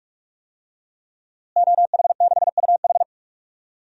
Morse code, heard as a keyed tone, about 44 wpm, 700 Hz. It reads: OH6UH
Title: DXCC-100 Call Signs - 5 Characters - Repeated 2x - 45wpm